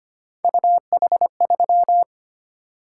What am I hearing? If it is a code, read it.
Morse code, heard as a keyed tone, about 25 wpm, 700 Hz. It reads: UH3
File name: Three Letter-Number Combinations - 25wpm